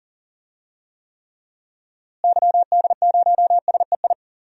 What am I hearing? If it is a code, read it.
Morse code, heard as a keyed tone, about 39 wpm, 700 Hz. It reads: YD0SEI
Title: DXCC-100 Call Signs - 6 Characters - Repeated 2x - 40wpm